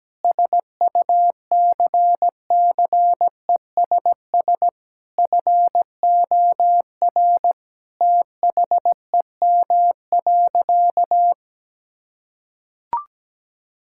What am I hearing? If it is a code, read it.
Morse code, heard as a keyed tone, about 17 wpm, 700 Hz. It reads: SUCCESS FOR THEM.
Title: Sentences from Top 1000 Words - Review Part 6 - 17wpm